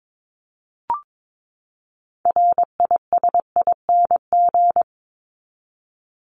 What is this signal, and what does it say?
Morse code, heard as a keyed tone, about 22 wpm, 700 Hz. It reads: RISING
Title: ING Suffix - Top 100 Words with Base Words 22wpm